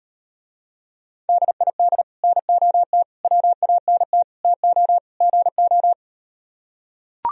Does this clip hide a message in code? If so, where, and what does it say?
Yes: Morse code from 1.3 to 5.9 s, heard as a keyed tone, about 38 wpm, 700 Hz. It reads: DID NOT WANT TO GO